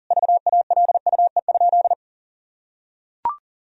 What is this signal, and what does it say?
Morse code, heard as a keyed tone, about 40 wpm, 700 Hz. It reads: VALUE?
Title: Sentences from Top 1000 Words - Review Part 3 - 40wpm